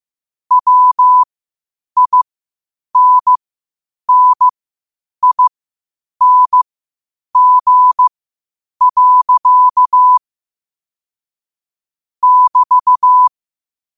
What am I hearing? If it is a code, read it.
Morse code, heard as a keyed tone, about 15 wpm, spaced as for 8 wpm, 1000 Hz. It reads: WINNING. <BT>
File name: qotdfiles / 8wpm